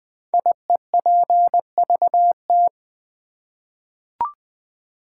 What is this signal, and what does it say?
Morse code, heard as a keyed tone, about 20 wpm, 700 Hz. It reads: IEPVT
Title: Code Groups - 5 Letters - 20wpm